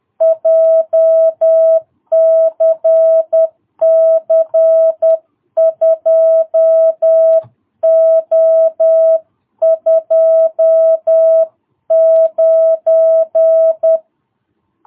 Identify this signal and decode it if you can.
Morse code, heard as a keyed tone, about 10 wpm, 660 Hz. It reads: JCC 2O29
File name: jh8cbh / 2022cw